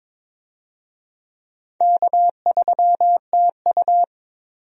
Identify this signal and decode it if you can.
Morse code, heard as a keyed tone, about 22 wpm, 700 Hz. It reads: K3TU
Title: USA Call Signs - 4 Characters - 22wpm